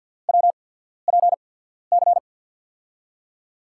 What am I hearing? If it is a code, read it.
Morse code, heard as a keyed tone, about 48 wpm, spaced as for 13 wpm, 700 Hz. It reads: WPC
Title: USA Call Signs - 6 Characters - 8x Character Spacing - 50wpm